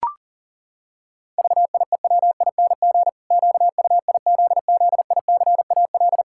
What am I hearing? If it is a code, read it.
Morse code, heard as a keyed tone, about 40 wpm, 700 Hz. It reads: VIEWING QUIZZICAL
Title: Pangram Challenge v2 - 40wpm